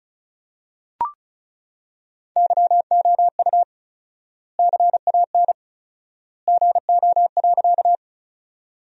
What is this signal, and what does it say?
Morse code, heard as a keyed tone, about 36 wpm, 700 Hz. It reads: YOU CAN GO.
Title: 4x Word-Spacing - Sentences from Top 100 Words - 35wpm